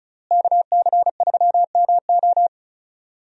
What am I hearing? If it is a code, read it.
Morse code, heard as a keyed tone, about 36 wpm, 700 Hz. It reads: KC3MO
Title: USA Call Signs - 5 Characters - Repeated 3x - 35wpm